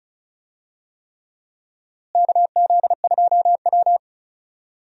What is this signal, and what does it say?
Morse code, heard as a keyed tone, about 36 wpm, 700 Hz. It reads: KZ2W
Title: USA Call Signs - 4 Characters - Repeated 3x - 35wpm